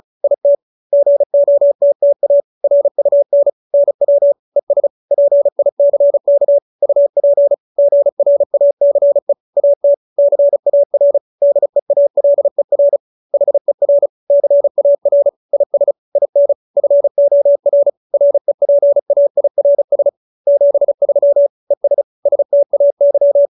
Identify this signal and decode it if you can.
Morse code, heard as a keyed tone, about 34 wpm, 570 Hz. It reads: IT GOTTA RUN NW ES PICK UP GRACE AT CAR DEALER HER CAR IS IN FOR REPAIRS 73 ES STAY
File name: QSO312 35